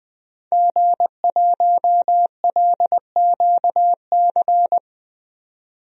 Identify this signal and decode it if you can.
Morse code, heard as a keyed tone, about 20 wpm, 700 Hz. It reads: G1LQC
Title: DXCC-100 Call Signs - 5 Characters - 20wpm